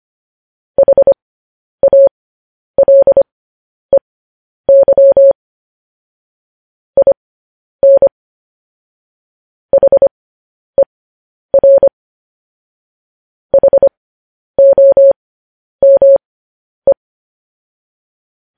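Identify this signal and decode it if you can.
Morse code, heard as a keyed tone, about 25 wpm, spaced as for 10 wpm, 550 Hz. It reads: HALEY IN HER HOME
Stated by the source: Morse code news headlines transmission.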